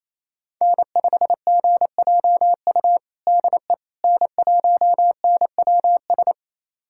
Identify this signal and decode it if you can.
Morse code, heard as a keyed tone, about 28 wpm, 700 Hz. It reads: N5GJU DE N1NWS
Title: USA Callsign DE Callsign - 5 Characters - 28wpm